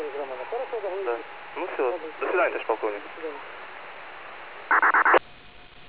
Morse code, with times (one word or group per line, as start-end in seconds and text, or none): none